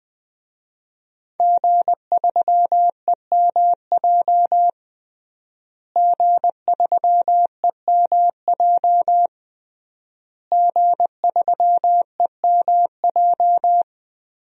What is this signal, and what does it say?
Morse code, heard as a keyed tone, about 20 wpm, 700 Hz. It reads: G3EMJ G3EMJ G3EMJ
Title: DXCC-100 Call Signs - 5 Characters - Repeated 3x - 20wpm